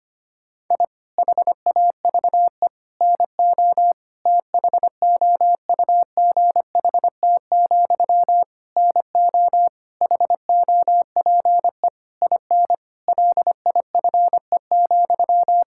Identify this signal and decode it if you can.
Morse code, heard as a keyed tone, about 25 wpm, 700 Hz. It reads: I HAVE NO THOUGHT, NO HOPE IN LIFE,
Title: Sentences from Top 1000 Words - Review Part 6 - 25wpm